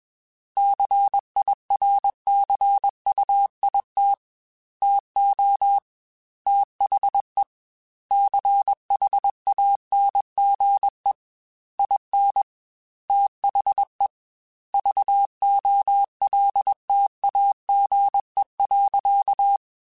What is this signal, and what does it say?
Morse code, heard as a keyed tone, about 21 wpm, 800 Hz. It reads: CIRCUIT TO THE CHANGE IN THE VOLTAGE.
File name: Technical Subjects / Technical Subjects-mp3s / XC @20